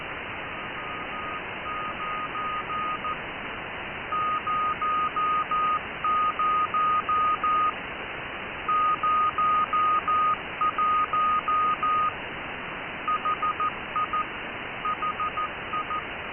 Morse code, heard as a keyed tone, about 14 wpm, 1240 Hz. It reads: M9 00 01 HI HI